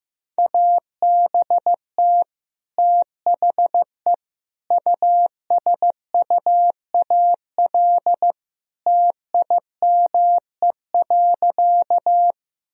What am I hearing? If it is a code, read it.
Morse code, heard as a keyed tone, about 15 wpm, 700 Hz. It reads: ABT THE USUAL TIME.
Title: Sentences from Top 400 Words and Abbreviations 15wpm